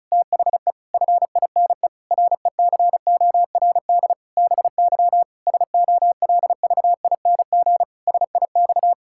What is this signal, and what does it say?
Morse code, heard as a keyed tone, about 36 wpm, 700 Hz. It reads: THE FINE RECORD BY SOLVING SIX
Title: Pangram Challenge v2 - 35wpm